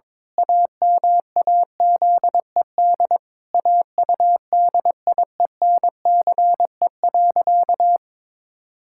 Morse code, heard as a keyed tone, about 22 wpm, 700 Hz. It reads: AMAZED AUDIENCE.
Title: Pangram Challenge v2 - 22wpm